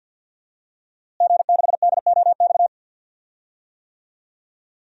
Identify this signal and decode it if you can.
Morse code, heard as a keyed tone, about 52 wpm, 700 Hz. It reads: G6DOX
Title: DXCC-100 Call Signs - 5 Characters - Repeated 3x - 50wpm